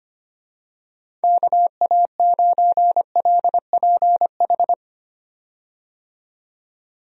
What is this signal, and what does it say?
Morse code, heard as a keyed tone, about 25 wpm, 700 Hz. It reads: KA9LPH
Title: USA Call Signs - 6 Characters - 25wpm